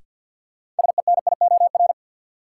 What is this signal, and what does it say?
Morse code, heard as a keyed tone, about 53 wpm, 700 Hz. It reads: SENIOR
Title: Sets of 5 Words Encore - New Word Drill - 50wpm